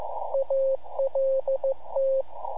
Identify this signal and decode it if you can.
Morse code, heard as a keyed tone, about 15 wpm, 540 Hz, with radio noise. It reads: ALT
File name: iHAB-3 / Repeater Recordings